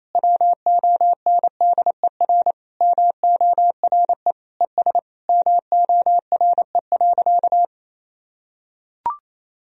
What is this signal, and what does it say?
Morse code, heard as a keyed tone, about 28 wpm, 700 Hz. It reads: WONDER MORE ES MORE.
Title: Sentences from Top 500 Words and Abbreviations 28wpm